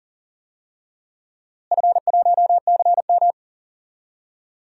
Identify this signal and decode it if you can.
Morse code, heard as a keyed tone, about 39 wpm, 700 Hz. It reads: F1CM
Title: DXCC-100 Call Signs - 4 Characters - Repeated 2x - 40wpm